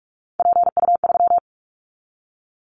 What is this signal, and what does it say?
Morse code, heard as a keyed tone, about 46 wpm, 700 Hz. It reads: PU3
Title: Callsign Prefixes for Top 50 DXCC - 45wpm